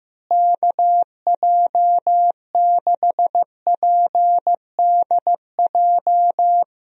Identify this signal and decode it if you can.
Morse code, heard as a keyed tone, about 15 wpm, 700 Hz. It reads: KJ6PDJ
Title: USA Call Signs - 6 Characters - Repeated 2x - 15wpm